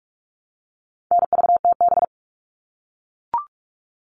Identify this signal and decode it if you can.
Morse code, heard as a keyed tone, about 44 wpm, 700 Hz. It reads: NVTB
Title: All Cut Numbers - 4-Digit 45wpm